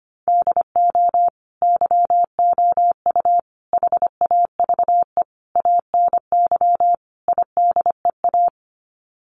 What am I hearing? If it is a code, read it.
Morse code, heard as a keyed tone, about 25 wpm, 700 Hz. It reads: DO YOU HAVE ANY IDEA
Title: Sets of 5 Words Encore 25wpm